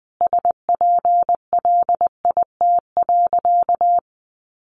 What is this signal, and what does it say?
Morse code, heard as a keyed tone, about 20 wpm, 700 Hz. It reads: SPLIT.
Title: Easy Phrases 20wpm